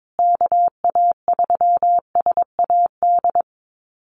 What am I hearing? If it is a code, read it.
Morse code, heard as a keyed tone, about 22 wpm, 700 Hz. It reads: KA3SAD
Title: USA Call Signs Repeated 2x 22wpm